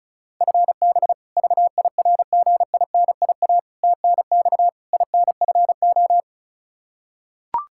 Morse code, heard as a keyed tone, about 36 wpm, 700 Hz. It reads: FB VIRGINIA TNX INFO
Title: K1USN SST QSO Practice 35wpm